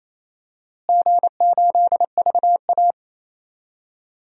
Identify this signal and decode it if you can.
Morse code, heard as a keyed tone, about 28 wpm, 700 Hz. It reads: G8VA
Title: DXCC-100 Call Signs - 4 Characters - Repeated 2x - 28wpm